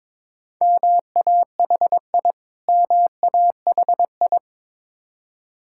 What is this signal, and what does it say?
Morse code, heard as a keyed tone, about 22 wpm, 690 Hz. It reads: MAHI MAHI
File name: Difficult Words with Hints - 22wpm